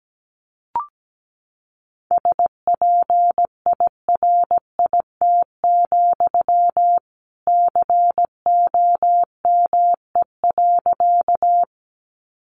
Morse code, heard as a keyed tone, about 17 wpm, 700 Hz. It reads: SPIRIT, COME.
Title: Sentences from Top 1400 Words 17wpm